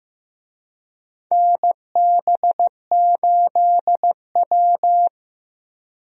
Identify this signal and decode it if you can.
Morse code, heard as a keyed tone, about 15 wpm, 700 Hz. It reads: NB8W
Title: USA Call Signs - 4 Characters - Repeated 2x - 15wpm